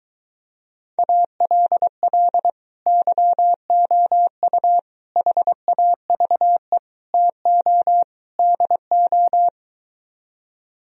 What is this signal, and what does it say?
Morse code, heard as a keyed tone, about 23 wpm, 700 Hz. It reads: ALL YOU HAVE TO DO